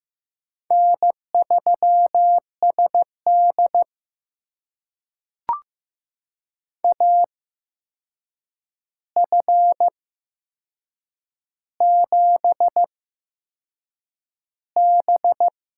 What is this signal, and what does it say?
Morse code, heard as a keyed tone, about 15 wpm, spaced as for 6 wpm, 700 Hz. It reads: N3SD AF7B
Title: USA Call Signs - 4 Characters - 8x Character Spacing - 15wpm